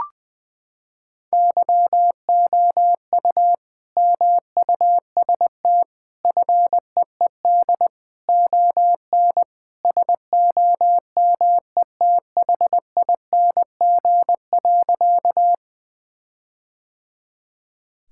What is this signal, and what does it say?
Morse code, heard as a keyed tone, about 20 wpm, 700 Hz. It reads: YOU MUST FEED ON SOMETHING.
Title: Sentences from Top 1600 Words 20wpm